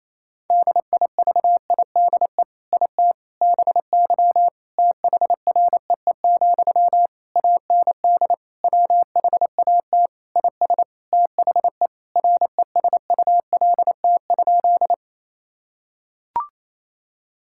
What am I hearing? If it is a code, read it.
Morse code, heard as a keyed tone, about 28 wpm, 700 Hz. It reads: DIVIDE IT BY THREE, AND WHAT IS THE RESULT?
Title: Sentences from Top 1600 Words 28wpm